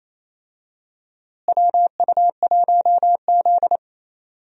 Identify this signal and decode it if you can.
Morse code, heard as a keyed tone, about 28 wpm, 700 Hz. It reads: WU1Z
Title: USA Call Signs - 4 Characters - Repeated 3x - 28wpm